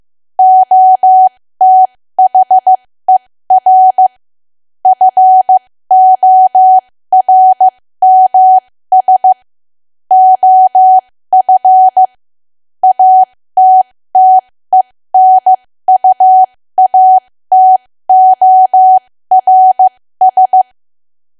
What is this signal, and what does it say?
Morse code, heard as a keyed tone, about 15 wpm, spaced as for 13 wpm, 750 Hz. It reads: OTHER FORMS OF ATTENUATORS
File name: cwa / W1AW-Level2 / 13WPM